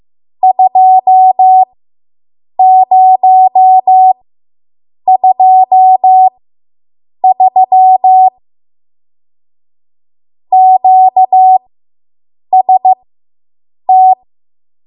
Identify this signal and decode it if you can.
Morse code, heard as a keyed tone, about 15 wpm, spaced as for 7 wpm, 750 Hz. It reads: 2023 QST